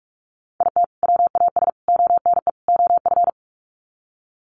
Morse code, heard as a keyed tone, about 44 wpm, 700 Hz. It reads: IT WAS ONE OF
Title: Sets of 4 Words 45wpm